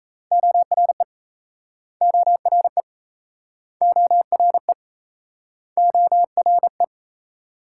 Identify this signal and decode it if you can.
Morse code, heard as a keyed tone, about 41 wpm, 700 Hz. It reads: ORE ORE ORE ORE